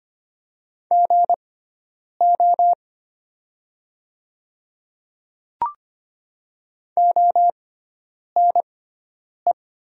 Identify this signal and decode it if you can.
Morse code, heard as a keyed tone, about 25 wpm, spaced as for 9 wpm, 700 Hz. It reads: GO ONE
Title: 6x Character Spacing - Top 100 Words - 25wpm